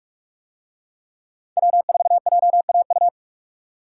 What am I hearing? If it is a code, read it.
Morse code, heard as a keyed tone, about 44 wpm, 700 Hz. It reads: W4JAU